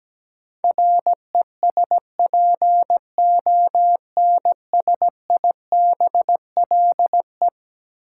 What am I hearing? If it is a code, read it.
Morse code, heard as a keyed tone, about 17 wpm, 700 Hz. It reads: RESPONSIBLE
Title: Sentences from Top 1000 Words - Review Part 7 - New Word Drill - 17wpm